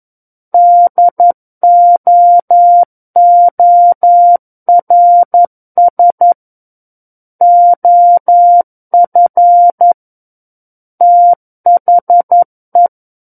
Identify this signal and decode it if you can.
Morse code, heard as a keyed tone, about 11 wpm, 700 Hz. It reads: DOORS OF THE
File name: Book One / 02-Chapter 2 - 11 WPM